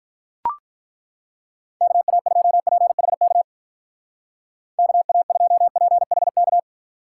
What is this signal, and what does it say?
Morse code, heard as a keyed tone, about 43 wpm, 700 Hz. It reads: KA2PHK KA2PHK
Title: Speed Racing - USA 2x3 Call Signs 35wpm